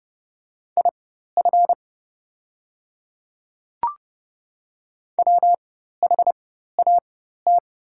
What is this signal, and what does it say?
Morse code, heard as a keyed tone, about 30 wpm, spaced as for 14 wpm, 700 Hz. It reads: IF WHAT
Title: 4x Character Spacing - Top 100 Words - 30wpm